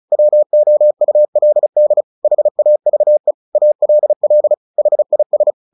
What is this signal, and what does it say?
Morse code, heard as a keyed tone, about 34 wpm, 600 Hz. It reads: WOULD HAVE ALL HIS
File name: the old man and the sea 2 35wpm